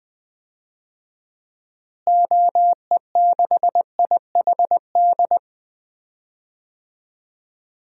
Morse code, heard as a keyed tone, about 20 wpm, 700 Hz. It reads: OE6IHD
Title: DXCC-100 Call Signs - 6 Characters - Repeated 2x - 20wpm